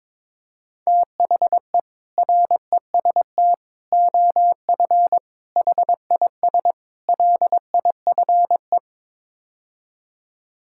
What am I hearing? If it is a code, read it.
Morse code, heard as a keyed tone, about 22 wpm, 700 Hz. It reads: THE REST OF HIS LIFE
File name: practice / 22wpm / Build A Phrase - 5-Word Phrases - 22wpm